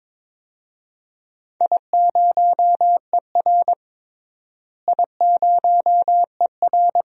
Morse code, heard as a keyed tone, about 21 wpm, 700 Hz. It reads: I0ER I0ER
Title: DXCC-100 Call Signs - 4 Characters - Repeated 2x - 22wpm